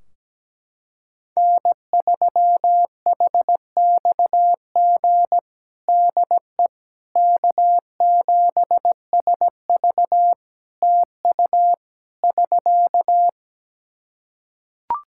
Morse code, heard as a keyed tone, about 17 wpm, 700 Hz. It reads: N3HXG DE K7SV TU <SK>
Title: K1USN SST QSO Practice 17wpm